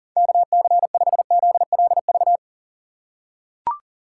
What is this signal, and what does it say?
Morse code, heard as a keyed tone, about 40 wpm, 700 Hz. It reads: KC5ZLV
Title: USA Call Signs - 6 Characters - 2x Character Spacing - 40wpm